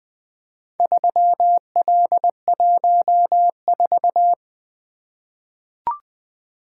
Morse code, heard as a keyed tone, about 20 wpm, 700 Hz. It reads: 3L14
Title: Four Letter-Number Combinations - 20wpm